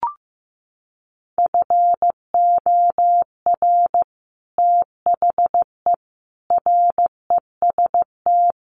Morse code, heard as a keyed tone, about 15 wpm, 700 Hz. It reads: FOR THE REST
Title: Sets of 5 Words Encore2 15wpm